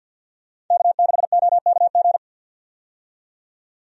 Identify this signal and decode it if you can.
Morse code, heard as a keyed tone, about 50 wpm, 700 Hz. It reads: K6OKG
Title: USA Call Signs - 5 Characters - 50wpm